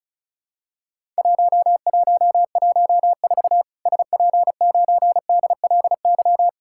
Morse code, heard as a keyed tone, about 36 wpm, 700 Hz. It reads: 1114 SP9DLY
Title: Worked All Europe DX Contest - QTC Exchange - 35wpm